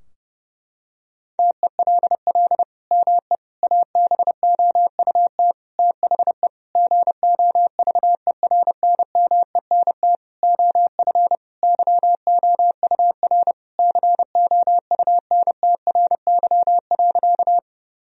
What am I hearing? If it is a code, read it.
Morse code, heard as a keyed tone, about 32 wpm, 700 Hz. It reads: TELL ME ABOUT THE GOVERNMENT OF YOUR COUNTRY.
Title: Sentences from Top 1000 Words - Review Part 3 - 30wpm